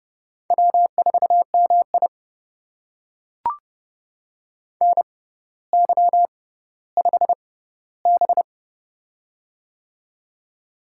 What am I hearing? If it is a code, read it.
Morse code, heard as a keyed tone, about 28 wpm, spaced as for 15 wpm, 700 Hz. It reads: W4MI NY5B
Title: USA Call Signs - 4 Characters - 6x Character Spacing - 30wpm